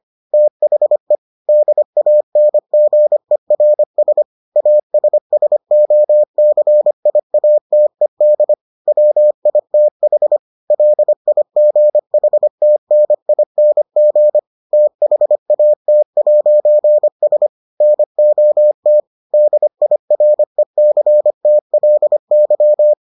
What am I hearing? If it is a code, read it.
Morse code, heard as a keyed tone, about 25 wpm, 600 Hz. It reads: THE DANGERS ASSOCIATED WITH LIGHTNING THAT'S NOT DIRECTLY